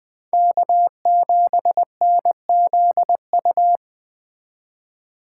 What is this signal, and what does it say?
Morse code, heard as a keyed tone, about 20 wpm, 700 Hz. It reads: K7NZU
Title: Speed Racing - USA 1x3 Call Signs 17wpm